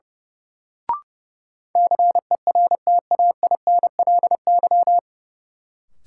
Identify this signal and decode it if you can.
Morse code, heard as a keyed tone, about 30 wpm, 700 Hz. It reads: CERTAINLY
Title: Sentences from Top 900 Words - New Word Drill - 30wpm